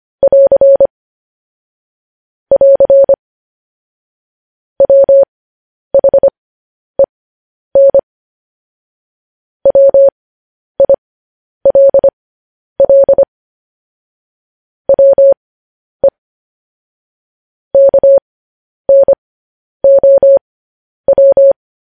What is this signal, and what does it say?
Morse code, heard as a keyed tone, about 25 wpm, spaced as for 10 wpm, 550 Hz. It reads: <AR> <AR> WHEN WILL WE KNOW
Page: news-headlines-in-morse-code-at-10-wpm